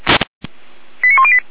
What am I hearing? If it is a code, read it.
Five-tone selective call (ZVEI), heard as quick runs of five tones, tones of about 70 ms. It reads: 89108